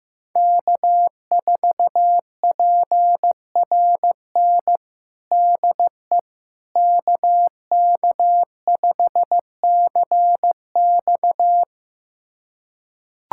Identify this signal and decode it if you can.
Morse code, heard as a keyed tone, about 15 wpm, 700 Hz. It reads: K4PRN DE KK5CX
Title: USA Callsign DE Callsign - 5 Characters - 15wpm